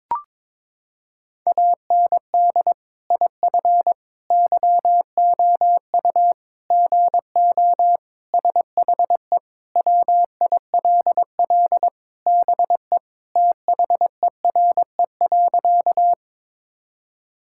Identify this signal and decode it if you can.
Morse code, heard as a keyed tone, about 22 wpm, 700 Hz. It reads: AND IF YOU GO SHE WILL BE THERE.
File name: practice / 22wpm / Sentences with 3-Word Phrases - 22wpm